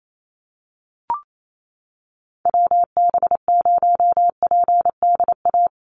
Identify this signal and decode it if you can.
Morse code, heard as a keyed tone, about 28 wpm, 700 Hz. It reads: WB0PDA
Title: USA Call Signs 28wpm